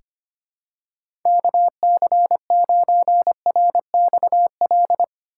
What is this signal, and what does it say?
Morse code, heard as a keyed tone, about 25 wpm, 700 Hz. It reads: KC9RXL
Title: USA Call Signs - 6 Characters - 2x Character Spacing - 25wpm